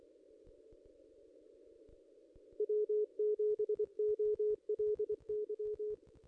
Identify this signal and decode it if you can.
Morse code, heard as a keyed tone, about 24 wpm, 420 Hz. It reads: W7OLY
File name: contests / ss-cw-k5tr-op-n5rz -1700